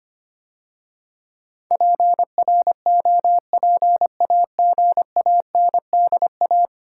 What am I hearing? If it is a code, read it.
Morse code, heard as a keyed tone, about 25 wpm, 700 Hz. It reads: PROPAGANDA
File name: 10 Letter Words - First 3 Letters Given - 25wpm